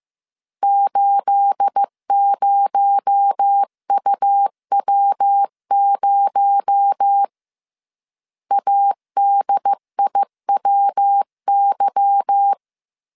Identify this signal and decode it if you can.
Morse code, heard as a keyed tone, about 15 wpm, 800 Hz. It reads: 80UW0 ADIWY
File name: CW / DET / 1 9 1500C S A 14wpm 3222